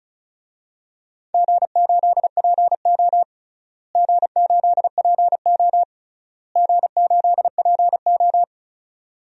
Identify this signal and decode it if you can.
Morse code, heard as a keyed tone, about 36 wpm, 700 Hz. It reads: G8PO G8PO G8PO
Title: DXCC-100 Call Signs - 4 Characters - Repeated 3x - 35wpm